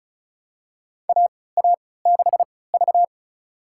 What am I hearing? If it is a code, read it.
Morse code, heard as a keyed tone, about 36 wpm, spaced as for 20 wpm, 700 Hz. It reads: AA6V
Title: USA Call Signs - 4 Characters - 3x Character Spacing - 35wpm